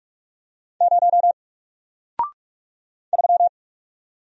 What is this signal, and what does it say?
Morse code, heard as a keyed tone, about 44 wpm, 700 Hz. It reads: 0 3
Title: Single Character (Letter:Number:Punctuation:Prosign) Alt 45wpm